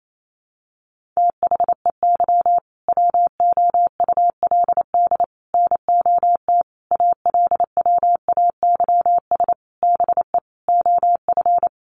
Morse code, heard as a keyed tone, about 28 wpm, 700 Hz. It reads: THEY WOULD NOT ALWAYS BE OF
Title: Sentences from Top 1400 Words 28wpm